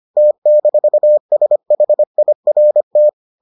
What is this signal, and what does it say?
Morse code, heard as a keyed tone, about 25 wpm, 600 Hz. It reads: T-SHIRT